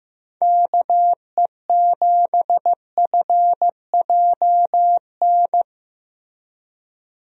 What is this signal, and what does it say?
Morse code, heard as a keyed tone, about 15 wpm, 700 Hz. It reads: KE7FJN
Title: USA Call Signs - 6 Characters - 15wpm